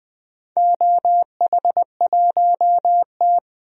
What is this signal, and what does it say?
Morse code, heard as a keyed tone, about 20 wpm, 700 Hz. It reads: OH1T
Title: DXCC-100 Call Signs - 4 Characters - 20wpm